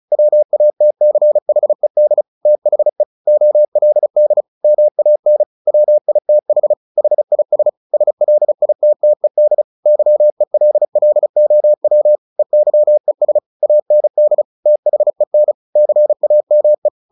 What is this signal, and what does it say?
Morse code, heard as a keyed tone, about 35 wpm, 600 Hz. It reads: WATCHED THE OLD MAN WITH HIS SLITTED YELLOW EYES AND THEN CAME
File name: the old man and the sea 2 35wpm